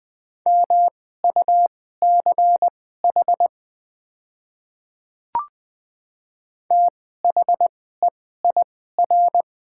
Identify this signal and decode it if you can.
Morse code, heard as a keyed tone, about 21 wpm, spaced as for 15 wpm, 700 Hz. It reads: MUCH THEIR
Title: 2x Character Spacing - Top 200 Words - 20wpm